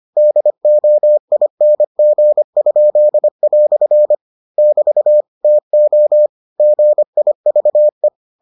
Morse code, heard as a keyed tone, about 25 wpm, 600 Hz. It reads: DOING?" <BT> TO GIVE